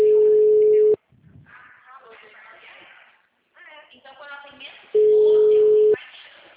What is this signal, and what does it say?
Ringback tone (Europe, 425 Hz), with speech.